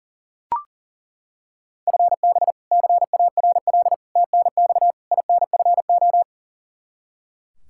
Morse code, heard as a keyed tone, about 40 wpm, 690 Hz. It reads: FB CARL TNX INFO
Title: Practice QSO - Exchange 3 - More Info - 40wpm